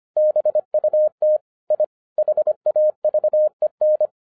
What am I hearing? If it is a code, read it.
Morse code, heard as a keyed tone, about 25 wpm, 610 Hz. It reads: BUT I HAVEN
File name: 2011-07-mp3 / n34 THE 3.11 2 25wpm